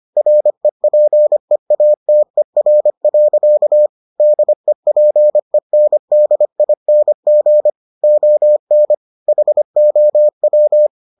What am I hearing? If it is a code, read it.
Morse code, heard as a keyed tone, about 25 wpm, 600 Hz. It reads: REPEATER. DEPENDING ON HOW